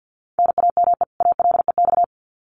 Morse code, heard as a keyed tone, about 50 wpm, 700 Hz. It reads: NAME ALEX
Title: Common QSO Phrases 50wpm